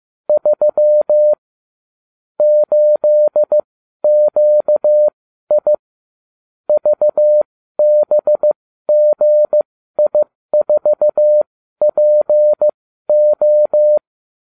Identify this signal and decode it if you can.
Morse code, heard as a keyed tone, about 15 wpm, spaced as for 12 wpm, 600 Hz. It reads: 3 8QI VBGI4PO